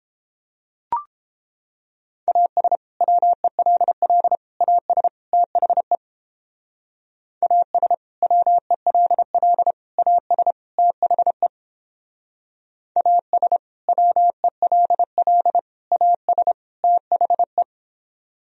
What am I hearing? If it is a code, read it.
Morse code, heard as a keyed tone, about 29 wpm, 700 Hz. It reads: AS WELL AS THE AS WELL AS THE AS WELL AS THE